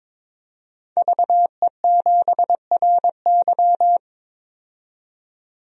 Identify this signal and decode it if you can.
Morse code, heard as a keyed tone, about 22 wpm, 700 Hz. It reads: VE7RY
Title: DXCC-100 Call Signs - 5 Characters - 22wpm